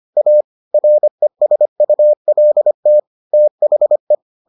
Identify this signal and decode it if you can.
Morse code, heard as a keyed tone, about 25 wpm, 600 Hz. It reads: A RESULT THE